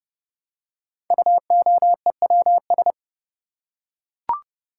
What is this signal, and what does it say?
Morse code, heard as a keyed tone, about 29 wpm, 690 Hz. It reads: UOEWS
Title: Code Groups - 5 Letters - 30wpm